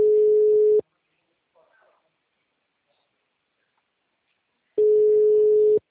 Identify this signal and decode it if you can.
Ringback tone (Europe, 425 Hz), with speech.